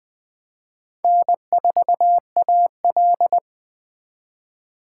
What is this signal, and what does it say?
Morse code, heard as a keyed tone, about 20 wpm, 700 Hz. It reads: N4AL
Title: Speed Racing - USA 1x2 Call Signs 17wpm